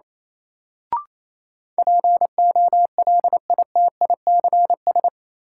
Morse code, heard as a keyed tone, about 28 wpm, 700 Hz. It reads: POLITICS
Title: Sentences from Top 1000 Words - New Word Drill - 28wpm